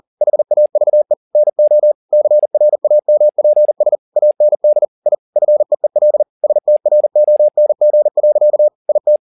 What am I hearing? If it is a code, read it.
Morse code, heard as a keyed tone, about 40 wpm, 600 Hz. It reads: HAVE NO CRAMPS AND I FEEL STRONG. IT